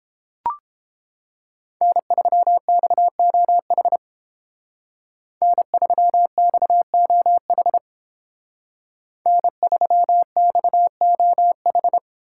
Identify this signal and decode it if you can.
Morse code, heard as a keyed tone, about 30 wpm, 700 Hz. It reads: N3XOH N3XOH N3XOH